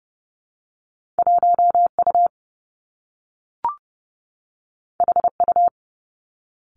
Morse code, heard as a keyed tone, about 30 wpm, 700 Hz. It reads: 1U HU
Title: Two Challenging Letter-Number Combinations 30wpm